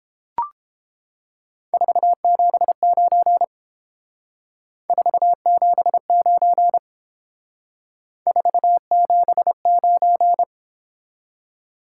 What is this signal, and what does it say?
Morse code, heard as a keyed tone, about 28 wpm, 700 Hz. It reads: 479 479 479